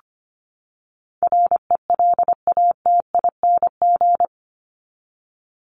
Morse code, heard as a keyed tone, about 25 wpm, 700 Hz. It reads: RELATING
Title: ING Suffix - Top 100 Words with Base Words 25wpm